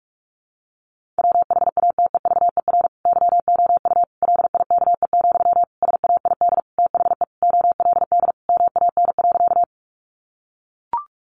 Morse code, heard as a keyed tone, about 44 wpm, 700 Hz. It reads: WHATEVER YOU LIKE, SAID THE OLD MAN.
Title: Sentences from Top 1100 Words 45wpm